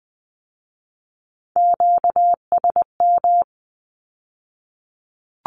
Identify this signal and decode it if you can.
Morse code, heard as a keyed tone, about 20 wpm, 700 Hz. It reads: QSM
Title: Q-Codes with Spoken Meaning - 20wpm